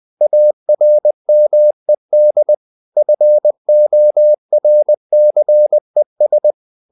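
Morse code, heard as a keyed tone, about 20 wpm, 600 Hz. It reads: ARMED FORCES
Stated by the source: Morse code at 19 wpm from CW med Gnister, reading Ken Follett's novel Never in English.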